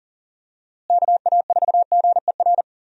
Morse code, heard as a keyed tone, about 40 wpm, 700 Hz. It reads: KA4GER
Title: USA Call Signs - 6 Characters - Repeated 2x - 40wpm